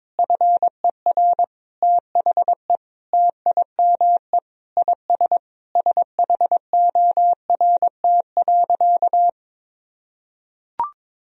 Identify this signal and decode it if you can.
Morse code, heard as a keyed tone, about 22 wpm, 700 Hz. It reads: FER THE TIME IS SHORT.
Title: Sentences from Top 500 Words and Abbreviations 22wpm